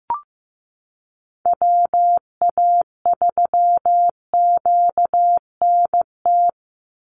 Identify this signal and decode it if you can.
Morse code, heard as a keyed tone, about 15 wpm, 700 Hz. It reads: WA3QNT
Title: Call Signs Repeated 3x - 15wpm